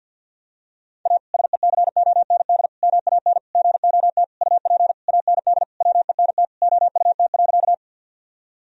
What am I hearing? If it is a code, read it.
Morse code, heard as a keyed tone, about 46 wpm, 700 Hz. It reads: A SECOND MAN GOT UP AND WENT OUT.
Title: Sentences from Top 1000 Words - Review Part 1 - 50wpm